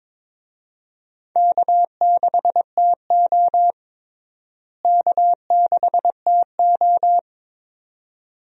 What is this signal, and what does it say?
Morse code, heard as a keyed tone, about 22 wpm, 700 Hz. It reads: K6TO K6TO
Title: USA Call Signs - 4 Characters - Repeated 2x - 22wpm